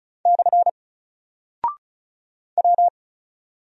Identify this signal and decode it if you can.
Morse code, heard as a keyed tone, about 36 wpm, 700 Hz. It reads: / W
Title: Single Character (Letter:Number:Punctuation:Prosign) Alt 35wpm